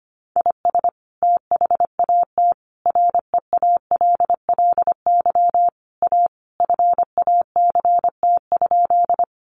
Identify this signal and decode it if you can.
Morse code, heard as a keyed tone, about 25 wpm, 700 Hz. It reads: IS THAT REALLY A FACT?
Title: Sentences from Top 600 Words 25wpm